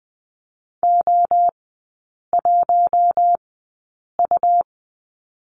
Morse code, heard as a keyed tone, about 20 wpm, spaced as for 8 wpm, 700 Hz. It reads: O1U
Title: 152 Lesson 25e - Letter p - Three Characters 20wpm